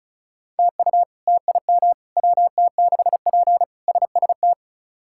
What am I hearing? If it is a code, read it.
Morse code, heard as a keyed tone, about 36 wpm, 700 Hz. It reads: TU TIM WT6P SST
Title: K1USN SST QSO Practice - 35wpm